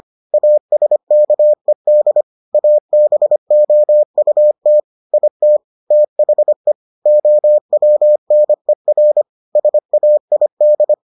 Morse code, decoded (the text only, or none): ASKED ABOUT IT THE OWNER SAID